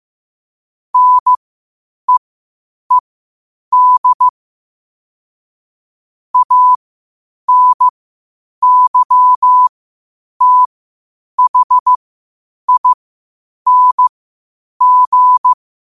Morse code, heard as a keyed tone, about 15 wpm, spaced as for 8 wpm, 1000 Hz. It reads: NEED ANYTHING
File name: qotdfiles / 8wpm